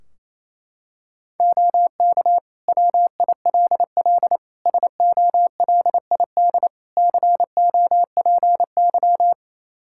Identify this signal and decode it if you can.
Morse code, heard as a keyed tone, about 28 wpm, 700 Hz. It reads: OK WILL SOLID COPY
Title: Practice QSO - Exchange 3 - More Info - 28wpm